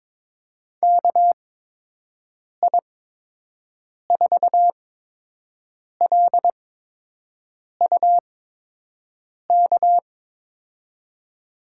Morse code, heard as a keyed tone, about 22 wpm, spaced as for 6 wpm, 700 Hz. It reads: KI4LUK
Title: USA Call Signs - 6 Characters - 8x Character Spacing - 22wpm